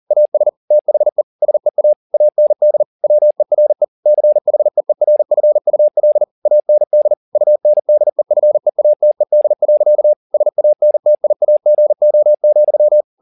Morse code, heard as a keyed tone, about 40 wpm, 600 Hz. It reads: AS THE SEA AND WERE CHEERFUL AND UNDEFEATED. SANTIAGO,